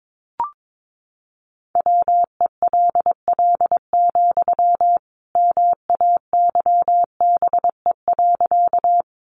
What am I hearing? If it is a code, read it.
Morse code, heard as a keyed tone, about 22 wpm, 700 Hz. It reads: WELL, MAYBE.
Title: Sentences from Top 1000 Words 22wpm